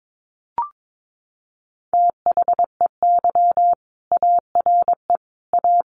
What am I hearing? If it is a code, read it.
Morse code, heard as a keyed tone, about 22 wpm, 700 Hz. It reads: THEY ARE A
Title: Sentences from Top 1200 Words 22wpm